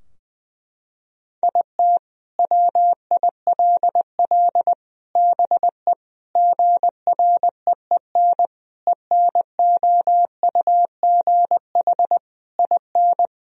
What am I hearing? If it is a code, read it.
Morse code, heard as a keyed tone, about 20 wpm, 700 Hz. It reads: IT WILL BE GREEN ENOUGH IN
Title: Sentences from Top 1000 Words - Review Part 2 - 20wpm